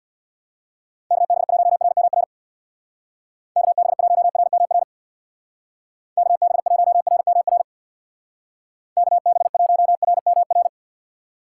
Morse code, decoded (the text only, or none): KB1RMR KB1RMR KB1RMR KB1RMR